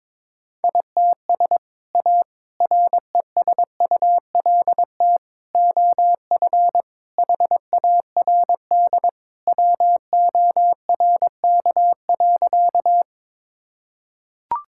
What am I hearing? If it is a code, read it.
Morse code, heard as a keyed tone, about 22 wpm, 700 Hz. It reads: ITS A RESULT OF HARD WORK.
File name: practice / 22wpm / Sentences with 3-Word Phrases - 22wpm